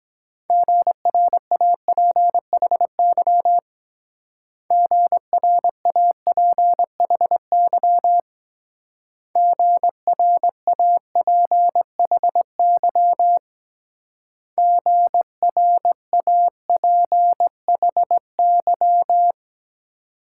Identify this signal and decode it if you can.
Morse code, heard as a keyed tone, about 22 wpm, 700 Hz. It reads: GRAPHY GRAPHY GRAPHY GRAPHY